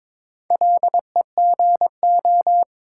Morse code, heard as a keyed tone, about 22 wpm, 690 Hz. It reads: LEGO
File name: Difficult Words with Hints - 22wpm